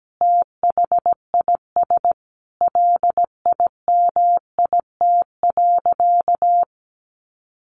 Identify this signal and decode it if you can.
Morse code, heard as a keyed tone, about 17 wpm, 700 Hz. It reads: THIS LIMIT.
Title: Sentences from Top 1500 Words 17wpm